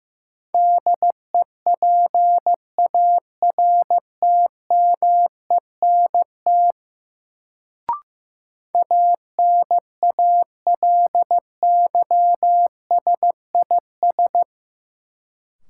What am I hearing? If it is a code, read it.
Morse code, heard as a keyed tone, about 15 wpm, 700 Hz. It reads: DEPARTMENT ANALYSIS
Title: Sentences from Top 1000 Words - Review Part 5 - New Word Drill - 15wpm